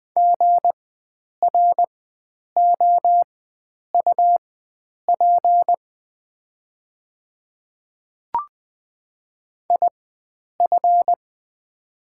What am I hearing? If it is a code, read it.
Morse code, heard as a keyed tone, about 20 wpm, spaced as for 9 wpm, 700 Hz. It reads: GROUP IF
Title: 4x Character Spacing - Top 300 Words - 20wpm